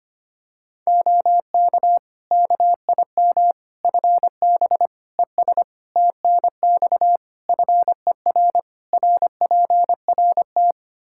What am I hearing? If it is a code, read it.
Morse code, heard as a keyed tone, about 25 wpm, 700 Hz. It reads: OK KIM FB ES TNX FER RPRT
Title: Practice QSO - Exchange 2 - Info - 25wpm